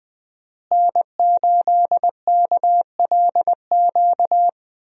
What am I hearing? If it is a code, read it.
Morse code, heard as a keyed tone, about 20 wpm, 700 Hz. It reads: N8KLQ
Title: USA Call Signs - 5 Characters - Repeated 3x - 20wpm